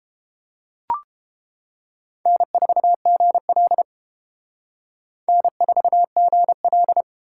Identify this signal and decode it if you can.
Morse code, heard as a keyed tone, about 28 wpm, 700 Hz. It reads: N4GL N4GL